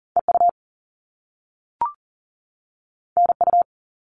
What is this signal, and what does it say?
Morse code, heard as a keyed tone, about 40 wpm, 700 Hz. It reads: EU NU
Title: All Cut Numbers - 2-Digit 40wpm